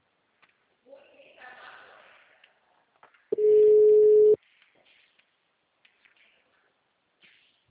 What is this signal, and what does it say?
Ringback tone (Europe, 425 Hz), with speech.